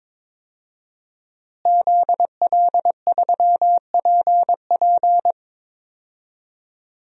Morse code, heard as a keyed tone, about 22 wpm, 700 Hz. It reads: ZL3PP
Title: DXCC-100 Call Signs - 5 Characters - Repeated 2x - 22wpm